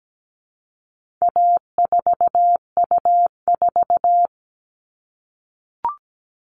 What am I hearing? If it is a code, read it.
Morse code, heard as a keyed tone, about 17 wpm, 700 Hz. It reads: A4U4
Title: All Cut Numbers - 4-Digit 17wpm